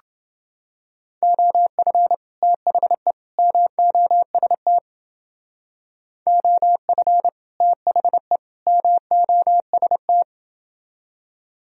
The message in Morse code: OF THE MOST OF THE MOST

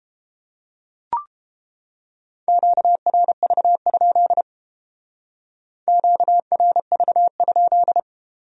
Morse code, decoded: QRV? QRV?

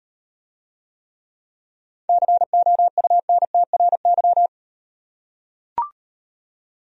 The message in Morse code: COUNTRY

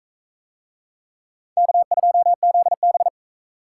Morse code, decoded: K2ZB

